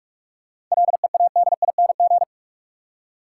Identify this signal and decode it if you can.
Morse code, heard as a keyed tone, about 47 wpm, 700 Hz. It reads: LEADING
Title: Seven Letter Words - 45wpm